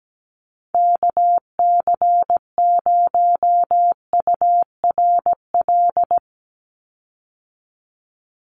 Morse code, heard as a keyed tone, about 17 wpm, 700 Hz. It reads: KC0URL
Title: Call Signs - 17wpm